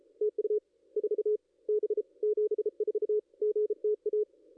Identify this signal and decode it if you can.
Morse code, heard as a keyed tone, about 36 wpm, 430 Hz. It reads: TU 4 B 74 GTA
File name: contests / ss-cw-k5tr-op-n5rz -2200